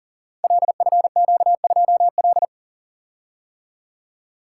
Morse code, heard as a keyed tone, about 40 wpm, 700 Hz. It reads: LFQ2L
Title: Code Groups - 5 Letter:Numbers - 40wpm